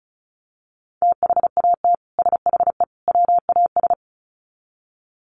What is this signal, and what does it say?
Morse code, heard as a keyed tone, about 34 wpm, 700 Hz. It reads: THAT SHE WAS
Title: Sets of 3 Words 35wpm